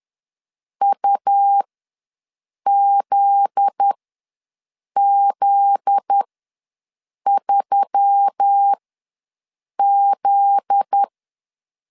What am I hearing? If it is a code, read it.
Morse code, heard as a keyed tone, about 10 wpm, spaced as for 6 wpm, 800 Hz. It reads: UZZ3Z
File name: LEZIONI / 7 GF9Z / 7 GF9Z ws 10wpm 3919